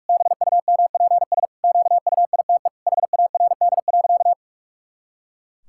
Morse code, heard as a keyed tone, about 46 wpm, 700 Hz. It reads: BUMPS QUITE HARD.
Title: Pangram Challenge v2 - 45wpm